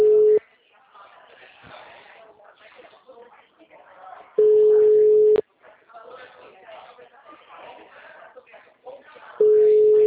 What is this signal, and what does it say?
Ringback tone (Europe, 425 Hz), with speech.